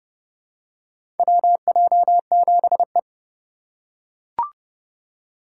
Morse code, heard as a keyed tone, about 29 wpm, 700 Hz. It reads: WJ7E
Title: USA Call Signs - 4 Characters - 3x Character Spacing - 30wpm